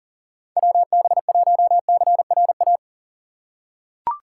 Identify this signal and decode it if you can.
Morse code, heard as a keyed tone, about 40 wpm, 700 Hz. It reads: WB1CRA
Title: USA Call Signs - 4x Character Spacing - 40wpm